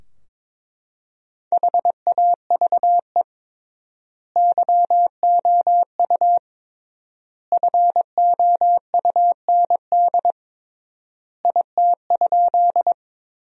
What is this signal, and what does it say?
Morse code, heard as a keyed tone, about 22 wpm, 700 Hz. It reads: HAVE YOU FOUND IT?
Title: 3x Word-Spacing - Sentences from Top 300 Words - 22wpm